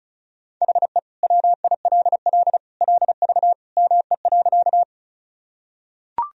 Morse code, heard as a keyed tone, about 36 wpm, 700 Hz. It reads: HE WILL LV ME.
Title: Sentences from Top 400 Words and Abbreviations 35wpm